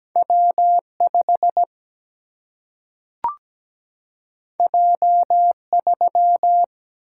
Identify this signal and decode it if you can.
Morse code, heard as a keyed tone, about 17 wpm, 700 Hz. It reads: W5 J3
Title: Single-Letter Single-Number 17wpm